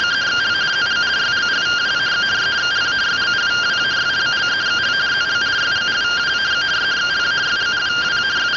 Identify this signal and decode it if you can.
Two-tone teleprinter signal, about 170 Hz shift.